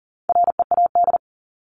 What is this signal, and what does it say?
Morse code, heard as a keyed tone, about 38 wpm, 700 Hz. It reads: READ
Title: Top 300 Words Repeated 40wpm